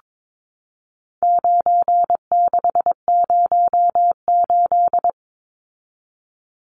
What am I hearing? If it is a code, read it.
Morse code, heard as a keyed tone, about 22 wpm, 700 Hz. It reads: 9608
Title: Numbers - 4-Digit 22wpm